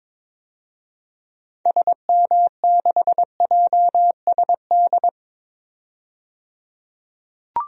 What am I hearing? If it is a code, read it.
Morse code, heard as a keyed tone, about 22 wpm, 700 Hz. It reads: SM6JSD
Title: DXCC-100 Call Signs - 6 Characters - 22wpm